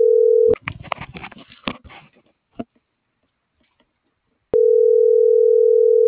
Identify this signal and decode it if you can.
Ringback tone (North America), 440 + 480 Hz.